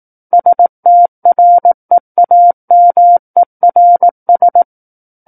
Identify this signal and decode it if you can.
Morse code, heard as a keyed tone, about 18 wpm, 700 Hz. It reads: STREAMERS
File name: 18 WPM